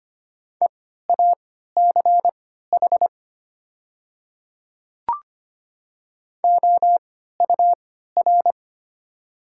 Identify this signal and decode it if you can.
Morse code, heard as a keyed tone, about 25 wpm, spaced as for 14 wpm, 700 Hz. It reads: EACH OUR
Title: 3x Character Spacing - Top 200 Words - 25wpm